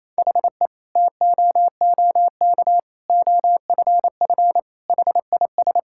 Morse code, heard as a keyed tone, about 28 wpm, 700 Hz. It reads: HE TOOK OFF HIS
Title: Sentences from Top 1600 Words 28wpm